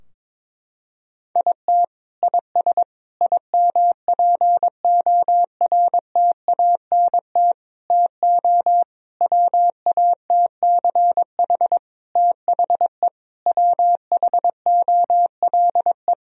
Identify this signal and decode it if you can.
Morse code, heard as a keyed tone, about 22 wpm, 700 Hz. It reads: IT IS IMPORTANT TO WATCH THE WHOLE
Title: Sentences from Top 1000 Words - Review Part 4 - 22wpm